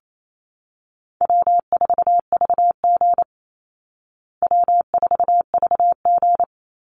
Morse code, heard as a keyed tone, about 28 wpm, 700 Hz. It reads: W4VG W4VG
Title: Easy Call Signs Repeated 2x - 28wpm